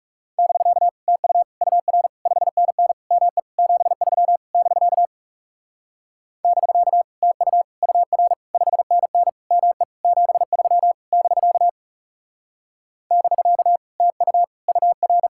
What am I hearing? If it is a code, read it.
Morse code, heard as a keyed tone, about 37 wpm, 700 Hz. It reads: <BK> TU UR 5NN ME 73 <BK> <BK> TU UR 5NN ME 73 <BK> <BK> TU UR
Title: Speed-Racing Common QSO Phrases Encore - 30wpm